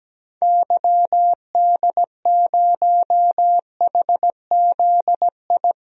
Morse code, heard as a keyed tone, about 17 wpm, 700 Hz. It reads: YD0HZI
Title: DXCC-100 Call Signs - 6 Characters - Repeated 3x - 17wpm